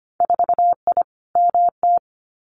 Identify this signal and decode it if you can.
Morse code, heard as a keyed tone, about 25 wpm, 700 Hz. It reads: 4I MT
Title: Winter Field Day Exchanges Encore 25wpm